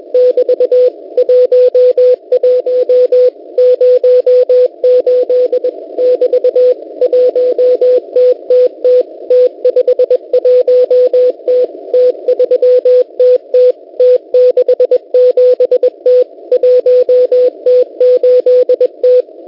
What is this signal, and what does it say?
Morse code, heard as a keyed tone, about 21 wpm, 500 Hz. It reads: <BT> 11 08 <BT> 1TTT T51T T3TT T67T 1T8T
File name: Unid- -0701utc-4735khz